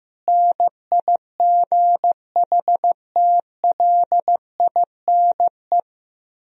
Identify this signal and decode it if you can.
Morse code, heard as a keyed tone, about 15 wpm, 700 Hz. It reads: NIGHTLINE
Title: Speed-Racing - Sets of 5 Words 15wpm